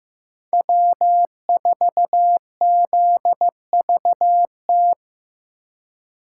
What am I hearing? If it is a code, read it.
Morse code, heard as a keyed tone, about 18 wpm, 700 Hz. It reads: W4ZVT
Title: Speed Racing - USA 1x3 Call Signs 15wpm